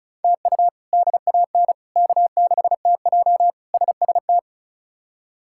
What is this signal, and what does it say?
Morse code, heard as a keyed tone, about 36 wpm, 700 Hz. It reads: TU DAN K6TJ SST
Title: K1USN SST QSO Practice - 35wpm